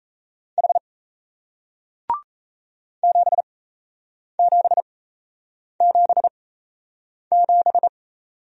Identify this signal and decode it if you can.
Morse code, heard as a keyed tone, about 38 wpm, spaced as for 8 wpm, 700 Hz. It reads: H 7777